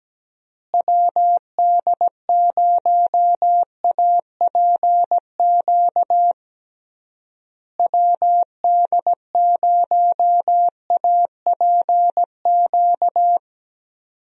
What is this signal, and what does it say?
Morse code, heard as a keyed tone, about 17 wpm, 700 Hz. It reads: WD0APQ WD0APQ
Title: USA Call Signs - 6 Characters - Repeated 3x - 17wpm